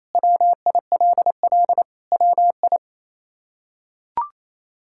Morse code, heard as a keyed tone, about 28 wpm, 700 Hz. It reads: WILL WI
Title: K1USN SST QSO Practice - 28wpm